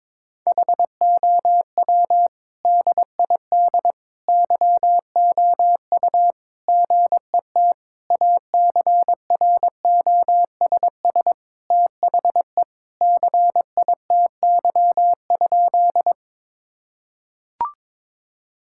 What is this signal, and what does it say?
Morse code, heard as a keyed tone, about 22 wpm, 700 Hz. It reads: HOW DID YOU GET ACROSS THE CITY?
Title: Sentences from Top 1000 Words - Review Part 5 - 22wpm